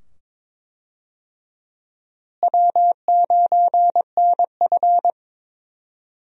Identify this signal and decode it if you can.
Morse code, heard as a keyed tone, about 22 wpm, 700 Hz. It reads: W9NF
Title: USA Call Signs - 4 Characters - Repeated 2x - 22wpm